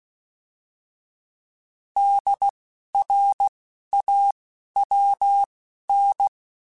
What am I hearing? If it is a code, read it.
Morse code, heard as a keyed tone, about 16 wpm, spaced as for 12 wpm, 800 Hz. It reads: DRAWN
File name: Alphabetical / Alphabetical mp3s / Ds @10